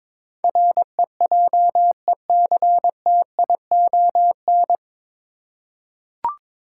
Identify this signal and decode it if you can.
Morse code, heard as a keyed tone, about 21 wpm, 700 Hz. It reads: REJECTION
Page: Morse Code Ninja practice 22wpm